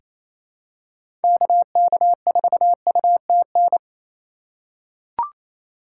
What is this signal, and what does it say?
Morse code, heard as a keyed tone, about 28 wpm, 700 Hz. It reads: KK4UTN
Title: USA Call Signs - 2x Character Spacing - 28wpm